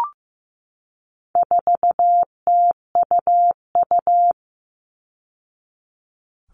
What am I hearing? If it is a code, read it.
Morse code, heard as a keyed tone, about 15 wpm, 700 Hz. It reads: 4TUU
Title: All Cut Numbers - 4-Digit 15wpm